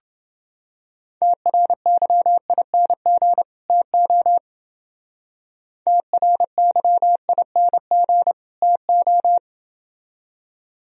Morse code, heard as a keyed tone, about 28 wpm, 700 Hz. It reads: TRYING TO TRYING TO